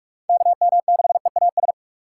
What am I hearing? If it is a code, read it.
Morse code, heard as a keyed tone, about 44 wpm, 700 Hz. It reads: KM6EAS
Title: USA Call Signs - 6x Character Spacing - 45wpm